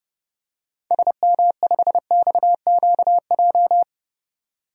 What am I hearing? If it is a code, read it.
Morse code, heard as a keyed tone, about 30 wpm, 700 Hz. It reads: SM5XQJ
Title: DXCC-100 Call Signs - 6 Characters - 30wpm